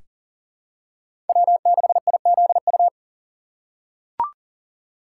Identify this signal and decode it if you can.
Morse code, heard as a keyed tone, about 40 wpm, 700 Hz. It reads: W6IZU
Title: USA Call Signs - 5 Characters - 3x Character Spacing - 40wpm